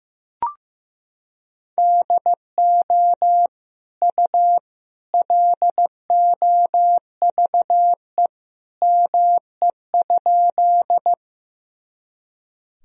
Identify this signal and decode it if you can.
Morse code, heard as a keyed tone, about 15 wpm, 700 Hz. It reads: DO U LOVE ME?
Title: Sentences from Top 400 Words and Abbreviations 15wpm